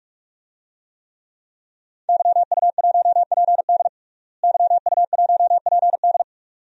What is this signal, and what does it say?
Morse code, heard as a keyed tone, about 48 wpm, 700 Hz. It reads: YU1PD YU1PD
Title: DXCC-100 Call Signs - 5 Characters - Repeated 2x - 45wpm